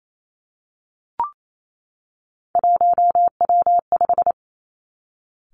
Morse code, heard as a keyed tone, about 28 wpm, 700 Hz. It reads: 1W5
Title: Three Challenging Letter-Number Combinations 28wpm